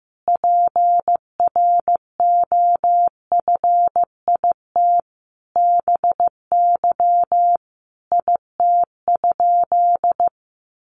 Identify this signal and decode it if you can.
Morse code, heard as a keyed tone, about 15 wpm, 700 Hz. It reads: PROFIT BY IT?
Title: Sentences from Top 800 Words 15wpm